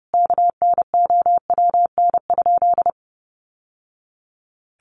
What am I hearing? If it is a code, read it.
Morse code, heard as a keyed tone, about 30 wpm, 700 Hz. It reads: KNOWN?
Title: Sentences from Top 700 Words 30wpm